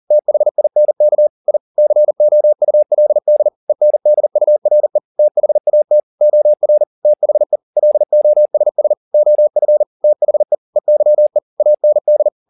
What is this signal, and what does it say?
Morse code, heard as a keyed tone, about 40 wpm, 600 Hz. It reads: THINK I COULD ENDURE THAT OR THE LOSS OF THE EYE AND